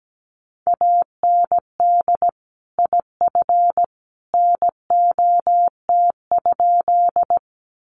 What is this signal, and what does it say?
Morse code, heard as a keyed tone, about 17 wpm, 700 Hz. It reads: AND IF NOT?
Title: Sentences from Top 600 Words 17wpm